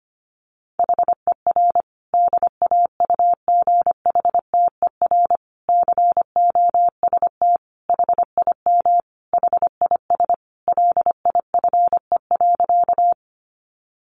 Morse code, heard as a keyed tone, about 25 wpm, 700 Hz. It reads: HER DAUGHTER COST HIM HIS LIFE.
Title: Sentences from Top 1500 Words 25wpm